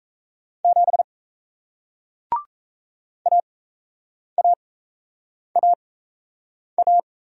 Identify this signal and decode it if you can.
Morse code, heard as a keyed tone, about 38 wpm, spaced as for 8 wpm, 700 Hz. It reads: 7 AAAA